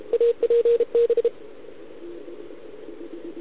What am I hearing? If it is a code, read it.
Morse code, heard as a keyed tone, about 33 wpm, 470 Hz. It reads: APB